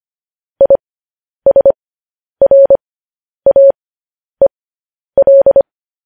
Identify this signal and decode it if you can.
Morse code, heard as a keyed tone, about 25 wpm, spaced as for 10 wpm, 550 Hz. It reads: ISRAEL
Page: news headlines in morse code at 10 wpm